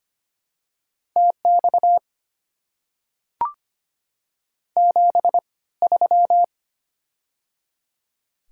Morse code, heard as a keyed tone, about 25 wpm, spaced as for 18 wpm, 690 Hz. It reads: TX 73
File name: Top 90 QSO Elements - 3x Character Spacing - 25wpm